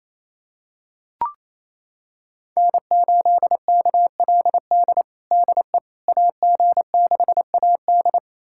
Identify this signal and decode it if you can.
Morse code, heard as a keyed tone, about 28 wpm, 700 Hz. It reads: N8KLD DE AG6AD
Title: USA Callsign DE Callsign - 5 Characters - 28wpm